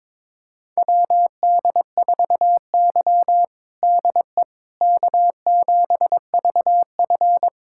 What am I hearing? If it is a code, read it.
Morse code, heard as a keyed tone, about 22 wpm, 700 Hz. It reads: WD4Y DE K7VF
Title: USA Callsign DE Callsign - 4 Characters - 22wpm